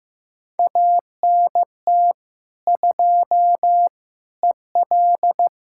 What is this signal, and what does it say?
Morse code, heard as a keyed tone, about 15 wpm, 700 Hz. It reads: ANT 2 EL
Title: Practice QSO - Exchange 2 - Info - 15wpm